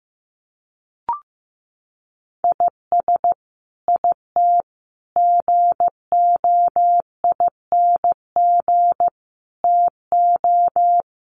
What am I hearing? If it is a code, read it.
Morse code, heard as a keyed tone, about 15 wpm, 700 Hz. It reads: IS IT GOING TO
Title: Sets of 5 Words Encore2 15wpm